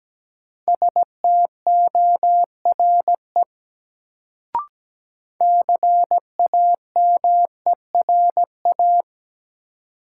Morse code, heard as a keyed tone, about 17 wpm, 700 Hz. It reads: STORE CAMERA
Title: Sentences from Top 1000 Words - Review Part 5 - New Word Drill - 17wpm